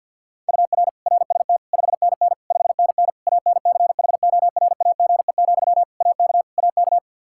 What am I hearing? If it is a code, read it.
Morse code, heard as a keyed tone, about 48 wpm, 700 Hz. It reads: UR RST 5NN 5NN ANCHORAGE, AK AK
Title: Practice QSO - Exchange 1 - Beginning - 50wpm